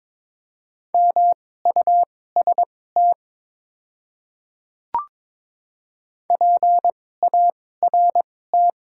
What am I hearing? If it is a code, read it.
Morse code, heard as a keyed tone, about 22 wpm, spaced as for 16 wpm, 700 Hz. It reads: MUST PART
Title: 2x Character Spacing - Top 200 Words - 22wpm